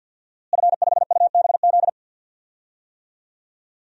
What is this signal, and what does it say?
Morse code, heard as a keyed tone, about 52 wpm, 700 Hz. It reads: F5UBZ
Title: DXCC-100 Call Signs - 5 Characters - Repeated 3x - 50wpm